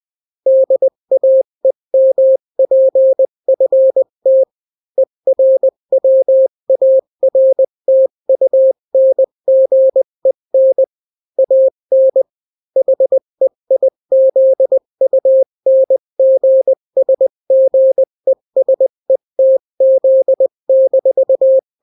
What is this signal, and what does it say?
Morse code, heard as a keyed tone, about 20 wpm, 530 Hz. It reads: DAEMPFT ERWARTUNGEN AN HEIZUNGSGESETZ-
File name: daily / 18wpm